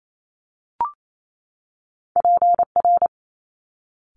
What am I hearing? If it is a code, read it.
Morse code, heard as a keyed tone, about 28 wpm, 700 Hz. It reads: PR
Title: Top Two Letter Combinations in Words 28wpm